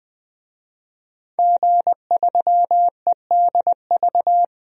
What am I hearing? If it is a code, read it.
Morse code, heard as a keyed tone, about 20 wpm, 700 Hz. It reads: G3EDV
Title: DXCC-100 Call Signs - 5 Characters - 20wpm